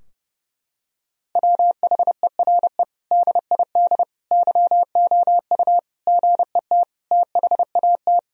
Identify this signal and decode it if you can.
Morse code, heard as a keyed tone, about 32 wpm, 700 Hz. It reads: WHERE DID YOU GET THAT
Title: Sentences from Top 1000 Words - Review Part 3 - 30wpm